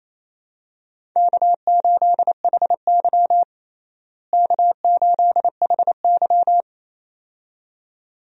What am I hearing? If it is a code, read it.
Morse code, heard as a keyed tone, about 28 wpm, 700 Hz. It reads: K8HY K8HY
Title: USA Call Signs - 4 Characters - Repeated 2x - 28wpm